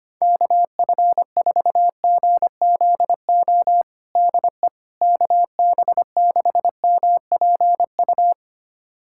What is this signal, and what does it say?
Morse code, heard as a keyed tone, about 25 wpm, 700 Hz. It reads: KF4GZO DE KB6MPU
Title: USA Callsign DE Callsign - 6 Characters - 25wpm